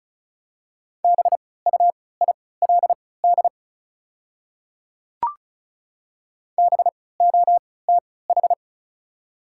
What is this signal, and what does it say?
Morse code, heard as a keyed tone, about 34 wpm, spaced as for 20 wpm, 700 Hz. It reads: BUILD BOTH
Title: 3x Character Spacing - Top 300 Words - 35wpm